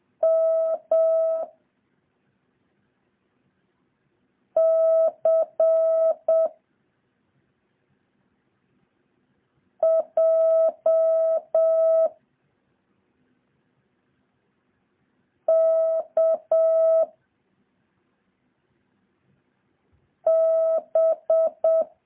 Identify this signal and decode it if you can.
Morse code, heard as a keyed tone, about 7 wpm, 650 Hz. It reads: M C J K B
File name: jh8cbh / 2023cw